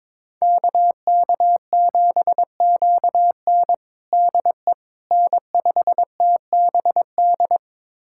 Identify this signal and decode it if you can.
Morse code, heard as a keyed tone, about 22 wpm, 700 Hz. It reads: KK7QN DE N5TBD
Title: USA Callsign DE Callsign - 5 Characters - 22wpm